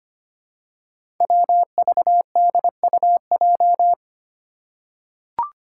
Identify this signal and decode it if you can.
Morse code, heard as a keyed tone, about 25 wpm, 700 Hz. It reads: WVDUJ
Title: Five Challenging Letter-Number Combinations 25wpm